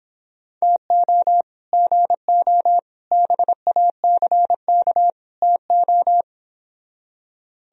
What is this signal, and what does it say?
Morse code, heard as a keyed tone, about 31 wpm, 700 Hz. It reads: TO GO BACK TO